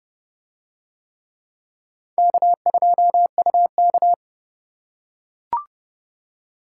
Morse code, heard as a keyed tone, about 28 wpm, 700 Hz. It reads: K2UK